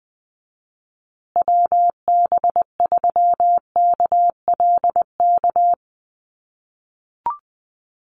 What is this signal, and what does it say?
Morse code, heard as a keyed tone, about 20 wpm, 700 Hz. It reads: WB3KLK
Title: USA Call Signs Repeated 3x 20wpm